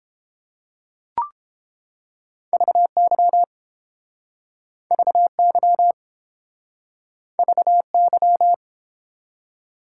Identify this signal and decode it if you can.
Morse code, heard as a keyed tone, about 30 wpm, 700 Hz. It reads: VY VY VY